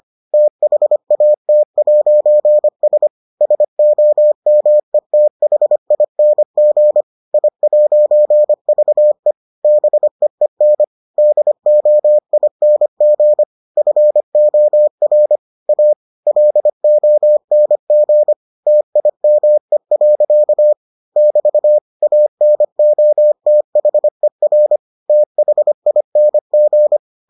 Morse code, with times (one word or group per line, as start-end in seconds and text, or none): 0.3-3.1 THAT'S
3.4-7.0 SOMETHING
7.3-9.3 I'VE
9.6-10.8 BEEN
11.2-13.4 DOING
13.8-15.4 FOR
15.7-15.9 A
16.3-18.3 LONG
18.7-20.7 TIME.
21.2-21.8 <BT>
22.0-24.8 ANOTHER
25.1-27.0 THING